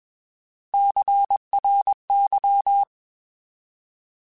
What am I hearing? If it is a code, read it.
Morse code, heard as a keyed tone, about 21 wpm, 800 Hz. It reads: CRY